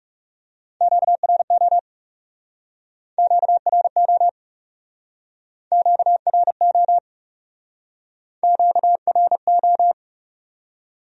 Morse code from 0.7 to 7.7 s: QRO QRO QRO

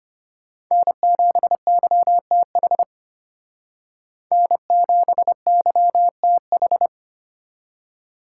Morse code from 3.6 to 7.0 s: N7YTH

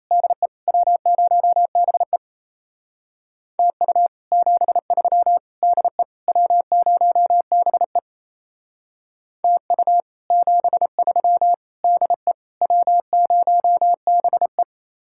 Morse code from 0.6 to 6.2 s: W0BE TU 73 DE

